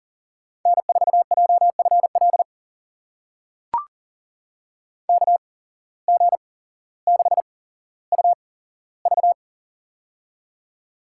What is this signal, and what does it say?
Morse code, heard as a keyed tone, about 40 wpm, spaced as for 17 wpm, 700 Hz. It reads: N4JFL KG6UV
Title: USA Call Signs - 5 Characters - 8x Character Spacing - 40wpm